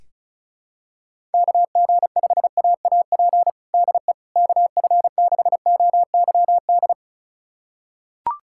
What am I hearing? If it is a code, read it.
Morse code, heard as a keyed tone, about 36 wpm, 700 Hz. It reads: KG5AAP DE KF6OYD
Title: USA Callsign DE Callsign - 6 Characters - 35wpm